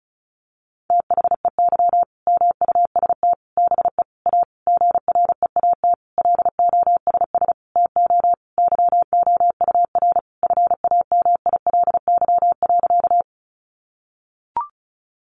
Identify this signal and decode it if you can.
Morse code, heard as a keyed tone, about 34 wpm, 700 Hz. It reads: THEY MUST BE A GREAT LOSS TO YOUR FAMILY.
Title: Sentences from Top 1400 Words 35wpm